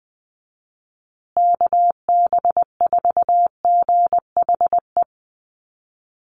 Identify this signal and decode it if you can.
Morse code, heard as a keyed tone, about 20 wpm, 700 Hz. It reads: KB4GHE
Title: USA Call Signs Repeated 3x 20wpm